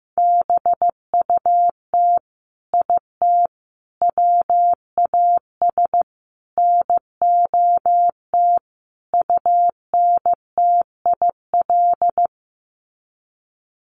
Morse code, heard as a keyed tone, about 15 wpm, 700 Hz. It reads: BUT IT WAS NOT UNTIL
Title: Sets of 5 Words Encore2 15wpm